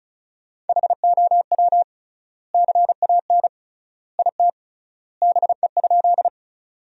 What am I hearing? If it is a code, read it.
Morse code, heard as a keyed tone, about 36 wpm, 700 Hz. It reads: HOW CAN IT BE?
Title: 3x Word-Spacing - Sentences from Top 100 Words - 35wpm